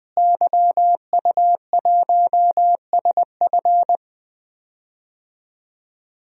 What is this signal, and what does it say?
Morse code, heard as a keyed tone, about 21 wpm, 700 Hz. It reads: YU1SF
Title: DXCC-100 Call Signs - 5 Characters - Repeated 2x - 20wpm